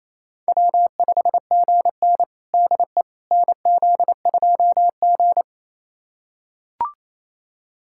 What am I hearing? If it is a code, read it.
Morse code, heard as a keyed tone, about 28 wpm, 700 Hz. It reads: W5GN DE NZ2G
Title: USA Callsign DE Callsign - 4 Characters - 28wpm